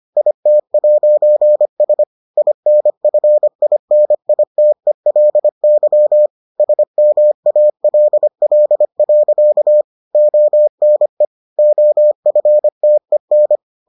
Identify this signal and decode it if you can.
Morse code, heard as a keyed tone, about 25 wpm, 600 Hz. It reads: IT'S INFINITELY SMALL. ONE OFTEN